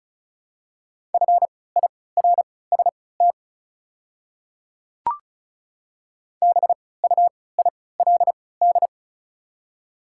Morse code, heard as a keyed tone, about 36 wpm, spaced as for 20 wpm, 690 Hz. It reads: FIRST BUILD
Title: 3x Character Spacing - Top 200 Words - 35wpm